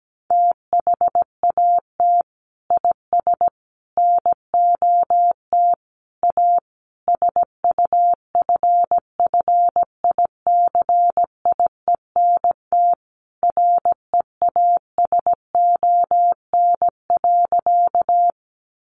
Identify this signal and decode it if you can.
Morse code, heard as a keyed tone, about 17 wpm, 700 Hz. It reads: THAT IS NOT A SUFFICIENT REASON.
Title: Sentences from Top 1400 Words 17wpm